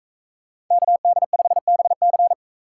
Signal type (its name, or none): Morse code